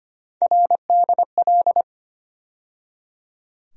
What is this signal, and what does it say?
Morse code, heard as a keyed tone, about 25 wpm, 700 Hz. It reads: RDL
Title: USA Call Sign Postfixes - 3 Letters 25wpm